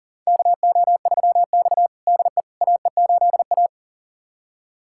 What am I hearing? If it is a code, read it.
Morse code, heard as a keyed tone, about 37 wpm, 700 Hz. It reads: KO3X DE AE8A
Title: USA Callsign DE Callsign - 4 Characters - 40wpm